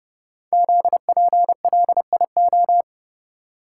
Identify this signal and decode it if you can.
Morse code, heard as a keyed tone, about 30 wpm, 700 Hz. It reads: ZPLIO
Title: Code Groups - 5 Letters - 30wpm